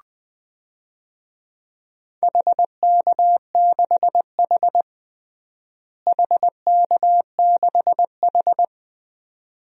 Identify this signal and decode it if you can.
Morse code, heard as a keyed tone, about 20 wpm, 700 Hz. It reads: HK6H HK6H
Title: DXCC-100 Call Signs - 4 Characters - Repeated 2x - 20wpm